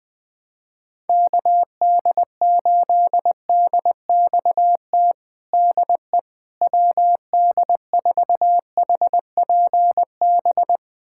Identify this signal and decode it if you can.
Morse code, heard as a keyed tone, about 20 wpm, 700 Hz. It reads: KD8DXT DE WD4HPB
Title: USA Callsign DE Callsign - 6 Characters - 20wpm